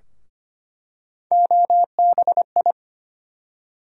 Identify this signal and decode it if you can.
Morse code, heard as a keyed tone, about 25 wpm, 700 Hz. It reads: OBI
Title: USA Call Sign Postfixes - 3 Letters 25wpm